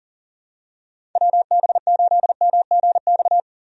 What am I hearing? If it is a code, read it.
Morse code, heard as a keyed tone, about 40 wpm, 700 Hz. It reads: WB8MGX
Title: USA Call Signs - 6 Characters - 40wpm